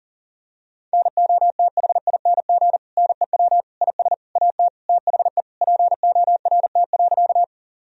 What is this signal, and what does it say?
Morse code, heard as a keyed tone, about 40 wpm, 700 Hz. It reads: NOTHING NEW IS AT THE PORT.
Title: Sentences from Top 1000 Words - Review Part 1 - 40wpm